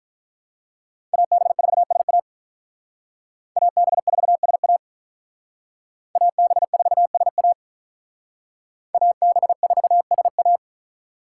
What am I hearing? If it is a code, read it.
Morse code, heard as a keyed tone, about 45 wpm, 700 Hz. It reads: AB4SA AB4SA AB4SA AB4SA